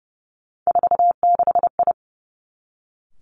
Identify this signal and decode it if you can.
Morse code, heard as a keyed tone, about 30 wpm, 700 Hz. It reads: 46I
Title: Three Challenging Letter-Number Combinations 30wpm